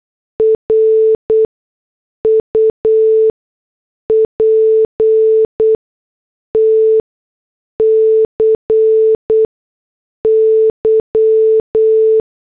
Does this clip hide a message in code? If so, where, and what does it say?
Yes: Morse code from 0.4 to 12.2 s, heard as a keyed tone, about 8 wpm, spaced as for 6 wpm, 440 Hz. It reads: RUPTCY